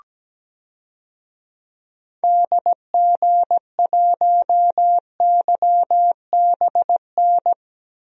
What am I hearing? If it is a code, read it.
Morse code, heard as a keyed tone, about 17 wpm, 700 Hz. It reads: DG1YBN
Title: DXCC-100 Call Signs - 6 Characters - Repeated 2x - 17wpm